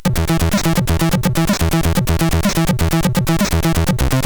There is no code.